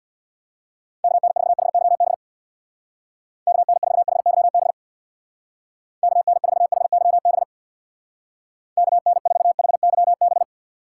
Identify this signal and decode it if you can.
Morse code, heard as a keyed tone, about 64 wpm, 700 Hz. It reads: KN4HYB KN4HYB KN4HYB KN4HYB